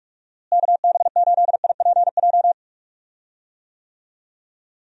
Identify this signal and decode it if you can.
Morse code, heard as a keyed tone, about 45 wpm, 700 Hz. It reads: KB8IPJ
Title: USA Call Signs - 6 Characters - Repeated 3x - 45wpm